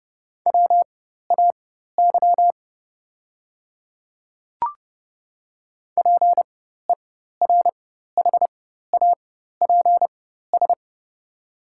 Morse code, heard as a keyed tone, about 30 wpm, spaced as for 14 wpm, 690 Hz. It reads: WAY PERHAPS
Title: 4x Character Spacing - Top 500 Words - 30wpm